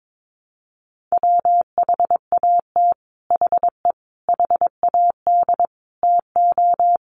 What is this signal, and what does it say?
Morse code, heard as a keyed tone, about 22 wpm, 700 Hz. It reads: WHAT HE HAD TO
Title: Sets of 5 Words Encore2 22wpm